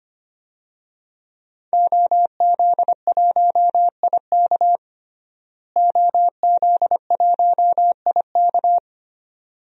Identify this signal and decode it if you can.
Morse code, heard as a keyed tone, about 25 wpm, 700 Hz. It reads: OZ1IK OZ1IK
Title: DXCC-100 Call Signs - 5 Characters - Repeated 2x - 25wpm